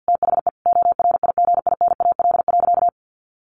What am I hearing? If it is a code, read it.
Morse code, heard as a keyed tone, about 50 wpm, 700 Hz. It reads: THE ORIGINAL.
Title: Sentences from Top 1400 Words 50wpm